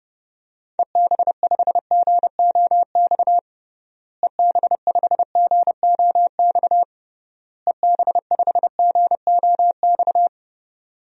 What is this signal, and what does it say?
Morse code, heard as a keyed tone, about 30 wpm, 700 Hz. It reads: EB5GOX EB5GOX EB5GOX
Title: DXCC-100 Call Signs - 6 Characters - Repeated 3x - 30wpm